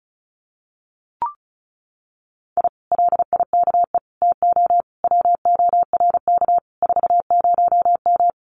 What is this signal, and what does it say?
Morse code, heard as a keyed tone, about 34 wpm, 700 Hz. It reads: I LIKE TO WORK 40M
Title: Common QSO Phrases Encore 35wpm